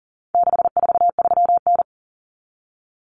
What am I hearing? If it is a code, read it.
Morse code, heard as a keyed tone, about 40 wpm, 700 Hz. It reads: 643N
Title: Common Cut Numbers - 4-Digit 40wpm